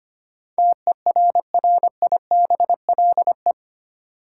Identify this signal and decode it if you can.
Morse code, heard as a keyed tone, about 25 wpm, 700 Hz. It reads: TERRIBLE
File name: Eight Letter Words - 25wpm